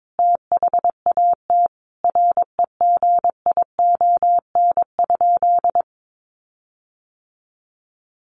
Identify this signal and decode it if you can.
Morse code, heard as a keyed tone, about 22 wpm, 700 Hz. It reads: THAT REGION?
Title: Sentences from Top 1500 Words 22wpm